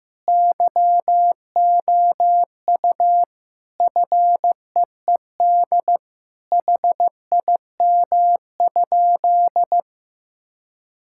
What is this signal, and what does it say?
Morse code, heard as a keyed tone, about 15 wpm, 700 Hz. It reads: YOU FEED HIM?
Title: Sentences from Top 1600 Words 15wpm